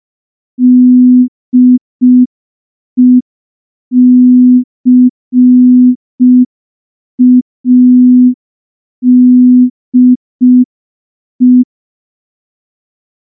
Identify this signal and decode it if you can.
Morse code, heard as a keyed tone, about 5 wpm, 250 Hz. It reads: DECADE
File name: morse / morse-training-05WPM